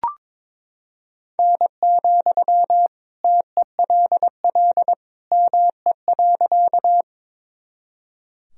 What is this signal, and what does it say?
Morse code, heard as a keyed tone, about 21 wpm, 700 Hz. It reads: N, TELL ME.
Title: Sentences from Top 200 Words and Abbreviations 22wpm